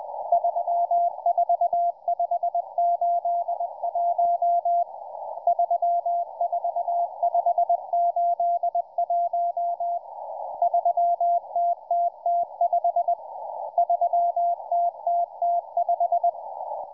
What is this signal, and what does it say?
Morse code, heard as a keyed tone, about 20 wpm, 690 Hz, with radio noise. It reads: 34581 34581 3TTT5 3TTT5